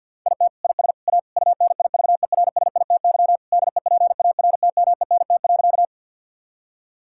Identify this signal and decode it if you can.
Morse code, heard as a keyed tone, about 50 wpm, 700 Hz. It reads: IT IS A UNIVERSITY DEPARTMENT.
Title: Sentences from Top 1000 Words - Review Part 5 - 50wpm